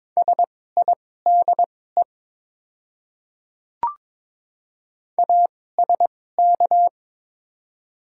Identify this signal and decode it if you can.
Morse code, heard as a keyed tone, about 21 wpm, spaced as for 16 wpm, 700 Hz. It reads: SIDE ASK
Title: 2x Character Spacing - Top 300 Words - 22wpm